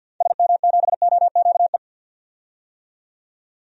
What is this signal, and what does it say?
Morse code, heard as a keyed tone, about 50 wpm, 700 Hz. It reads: SM7OQE
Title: DXCC-100 Call Signs - 6 Characters - 50wpm